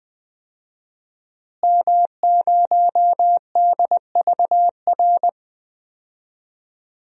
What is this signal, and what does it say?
Morse code, heard as a keyed tone, about 20 wpm, 700 Hz. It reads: M0DVR
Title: DXCC-100 Call Signs - 5 Characters - Repeated 2x - 20wpm